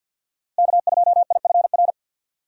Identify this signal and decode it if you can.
Morse code, heard as a keyed tone, about 48 wpm, 700 Hz. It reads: K2IFR
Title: USA Call Signs - 5 Characters - Repeated 3x - 50wpm